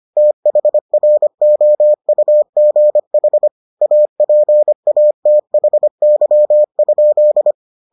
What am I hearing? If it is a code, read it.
Morse code, heard as a keyed tone, about 25 wpm, 600 Hz. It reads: THROUGH APATHY?